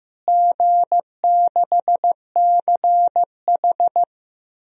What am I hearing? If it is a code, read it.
Morse code, heard as a keyed tone, about 15 wpm, 700 Hz. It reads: G6CH
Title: DXCC-100 Call Signs - 4 Characters - Repeated 2x - 15wpm